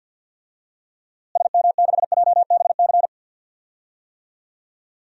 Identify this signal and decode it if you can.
Morse code, heard as a keyed tone, about 47 wpm, 700 Hz. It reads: SM6JBC
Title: DXCC-100 Call Signs - 6 Characters - Repeated 2x - 50wpm